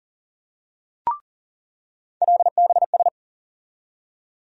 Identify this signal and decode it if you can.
Morse code, heard as a keyed tone, about 40 wpm, 700 Hz. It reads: LBS
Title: USA Call Sign Postfixes - 3 Letters 40wpm